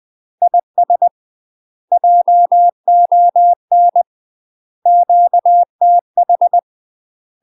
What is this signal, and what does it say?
Morse code, heard as a keyed tone, about 20 wpm, 700 Hz. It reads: IS JON QTH
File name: UK MASTER QSOs TRAINING / 20WPM / UK MASTER QSOs 0068